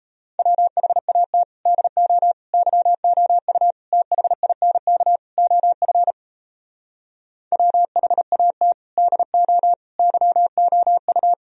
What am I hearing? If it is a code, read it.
Morse code, heard as a keyed tone, about 37 wpm, 700 Hz. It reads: WHAT DO YOU THINK OF WHAT DO YOU